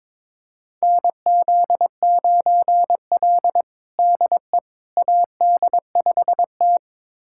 Morse code, heard as a keyed tone, about 22 wpm, 700 Hz. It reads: NZ9L DE AD5T
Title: USA Callsign DE Callsign - 4 Characters - 22wpm